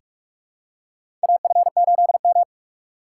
Morse code, heard as a keyed tone, about 46 wpm, 700 Hz. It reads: AF8M
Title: USA Call Signs - 4 Characters - 4x Character Spacing - 45wpm